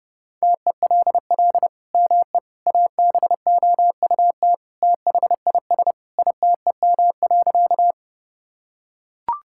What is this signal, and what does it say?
Morse code, heard as a keyed tone, about 30 wpm, 700 Hz. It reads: TELL ME ABOUT THIS ITEM.
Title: Sentences from Top 1000 Words - Review Part 6 - 30wpm